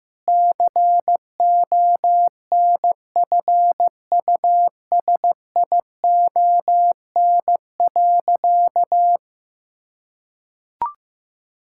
Morse code, heard as a keyed tone, about 14 wpm, 700 Hz. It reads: CONFUSION.
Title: Sentences from Top 1600 Words 15wpm